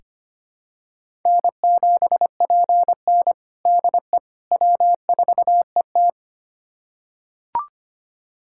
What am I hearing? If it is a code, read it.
Morse code, heard as a keyed tone, about 25 wpm, 700 Hz. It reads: N7PN DE W4ET
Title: USA Callsign DE Callsign - 4 Characters - 25wpm